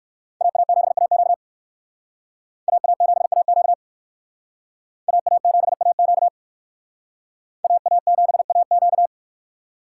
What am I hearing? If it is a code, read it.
Morse code, heard as a keyed tone, about 56 wpm, 700 Hz. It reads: AA7AQ AA7AQ AA7AQ AA7AQ
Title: Speed Racing - USA 2x2 Call Signs 45wpm